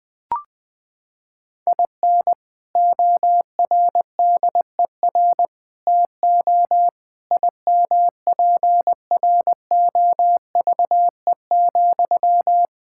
Morse code, heard as a keyed tone, about 20 wpm, 700 Hz. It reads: IN ORDER TO IMPROVE,
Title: Sentences from Top 1000 Words - Review Part 5 - 20wpm